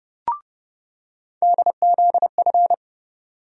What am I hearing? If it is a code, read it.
Morse code, heard as a keyed tone, about 31 wpm, 700 Hz. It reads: DZF
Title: USA Call Sign Postfixes - 3 Letters 30wpm